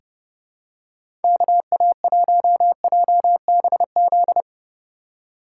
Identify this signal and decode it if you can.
Morse code, heard as a keyed tone, about 30 wpm, 700 Hz. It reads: KA1JBZ
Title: USA Call Signs - 6 Characters - 2x Character Spacing - 30wpm